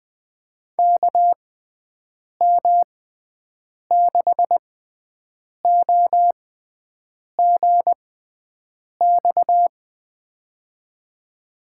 Morse code, heard as a keyed tone, about 20 wpm, spaced as for 7 wpm, 700 Hz. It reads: KM6OGX
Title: USA Call Signs - 6 Characters - 6x Character Spacing - 20wpm